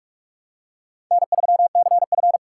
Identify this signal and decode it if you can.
Morse code, heard as a keyed tone, about 44 wpm, 700 Hz. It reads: N3CF
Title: USA Call Signs - 4 Characters - 4x Character Spacing - 45wpm